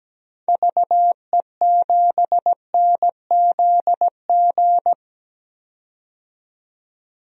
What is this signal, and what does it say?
Morse code, heard as a keyed tone, about 17 wpm, 700 Hz. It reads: VE7NZG
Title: DXCC-100 Call Signs - 6 Characters - 17wpm